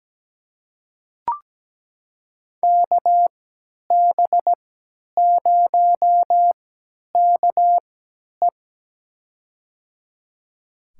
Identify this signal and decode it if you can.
Morse code, heard as a keyed tone, about 17 wpm, spaced as for 10 wpm, 700 Hz. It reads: KB0KE
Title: USA Call Signs - 5 Characters - 3x Character Spacing - 17wpm